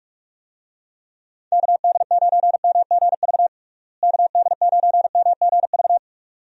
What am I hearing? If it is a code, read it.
Morse code, heard as a keyed tone, about 46 wpm, 700 Hz. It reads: KD9MGV KD9MGV
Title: USA Call Signs - 6 Characters - Repeated 2x - 45wpm